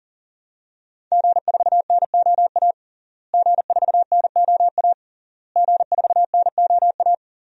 Morse code, heard as a keyed tone, about 37 wpm, 700 Hz. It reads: G4NOA G4NOA G4NOA
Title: DXCC-100 Call Signs - 5 Characters - Repeated 3x - 40wpm